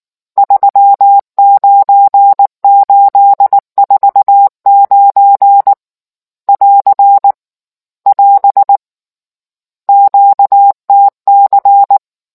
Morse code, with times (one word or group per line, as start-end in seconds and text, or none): 0.4-5.7 39849
6.5-7.3 <AR>
8.1-8.8 <AS>
9.9-12.0 QTC